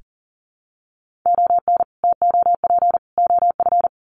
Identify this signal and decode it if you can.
Morse code, heard as a keyed tone, about 40 wpm, 700 Hz. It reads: ON TOP OF
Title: Sets of 3 Words 40wpm